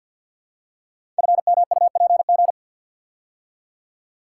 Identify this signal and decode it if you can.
Morse code, heard as a keyed tone, about 48 wpm, 700 Hz. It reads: FMUPG
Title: Code Groups - 5 Letters - 50wpm